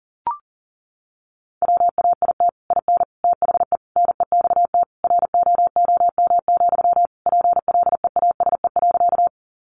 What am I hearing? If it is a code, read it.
Morse code, heard as a keyed tone, about 40 wpm, 700 Hz. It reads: WAIT IN THE NEXT ROOM, PLEASE.
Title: Sentences from Top 1500 Words 40wpm